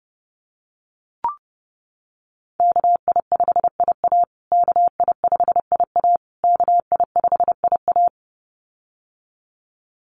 Morse code, heard as a keyed tone, about 28 wpm, 700 Hz. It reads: KI5IA KI5IA KI5IA
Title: USA Call Signs Repeated 3x 30wpm